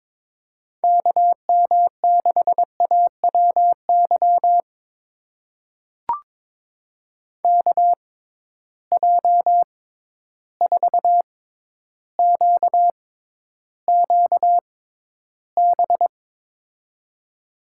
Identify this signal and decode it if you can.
Morse code, heard as a keyed tone, about 21 wpm, spaced as for 11 wpm, 700 Hz. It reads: KM6AWY KJ4QQB
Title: USA Call Signs - 6 Characters - 6x Character Spacing - 22wpm